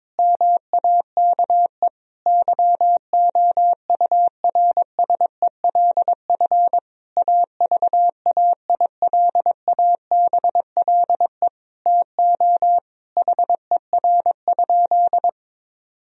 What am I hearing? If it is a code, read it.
Morse code, heard as a keyed tone, about 22 wpm, 700 Hz. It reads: MAKE YOURSELF AVAILABLE TO HER?
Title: Sentences from Top 1000 Words - Review Part 3 - 22wpm